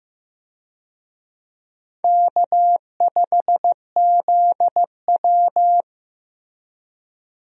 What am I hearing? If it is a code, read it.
Morse code, heard as a keyed tone, about 16 wpm, 700 Hz. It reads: K5ZW
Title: USA Call Signs - 4 Characters - Repeated 3x - 15wpm